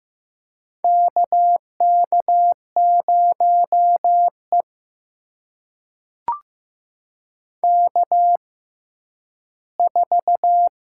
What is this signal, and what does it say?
Morse code, heard as a keyed tone, about 15 wpm, spaced as for 10 wpm, 700 Hz. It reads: KK0E K4
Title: USA Call Signs - 4 Characters - 6x Character Spacing - 15wpm